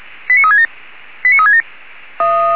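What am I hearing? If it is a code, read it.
Five-tone selective call (ZVEI), heard as quick runs of five tones, tones of about 70 ms. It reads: 89367 89367
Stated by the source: Five-tone ZVEI selective call, address 89366, received on an SDR.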